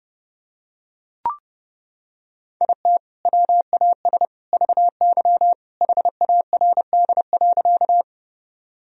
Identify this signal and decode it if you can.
Morse code, heard as a keyed tone, about 30 wpm, 700 Hz. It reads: IT WAS VY HARD.
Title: Sentences from Top 300 Words and Abbreviations 30wpm